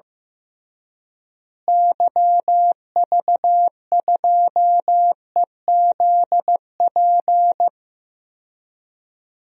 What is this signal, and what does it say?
Morse code, heard as a keyed tone, about 16 wpm, 700 Hz. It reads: YV2EZP
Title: DXCC-100 Call Signs - 6 Characters - Repeated 3x - 15wpm